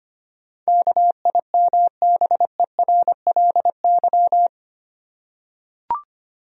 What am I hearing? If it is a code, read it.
Morse code, heard as a keyed tone, about 25 wpm, 700 Hz. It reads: KIMBERLY
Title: USA Common Names - 25wpm